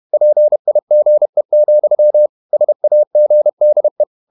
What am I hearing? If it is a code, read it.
Morse code, heard as a keyed tone, about 31 wpm, 600 Hz. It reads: PIGE, SAGDE
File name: Ferie QRQ 31wpm